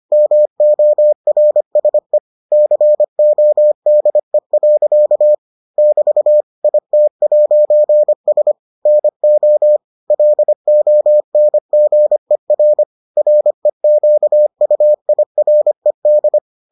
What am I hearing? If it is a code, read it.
Morse code, heard as a keyed tone, about 25 wpm, 600 Hz. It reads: MORSE CODE. <BT> IT'S NO LONGER REQUIRED